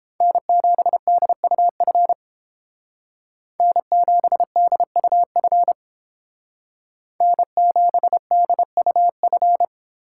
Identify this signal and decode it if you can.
Morse code, heard as a keyed tone, about 30 wpm, 700 Hz. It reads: N7DUF N7DUF N7DUF